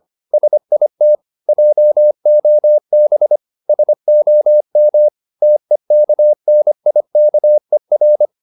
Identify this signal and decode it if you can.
Morse code, heard as a keyed tone, about 25 wpm, 600 Hz. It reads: SIT JOB SOM TEKNIKER
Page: Ferie 25 wpm